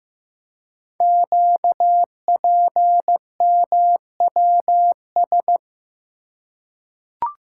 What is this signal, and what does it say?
Morse code, heard as a keyed tone, about 15 wpm, 700 Hz. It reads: QPMWS
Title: Code Groups - 5 Letters - 15wpm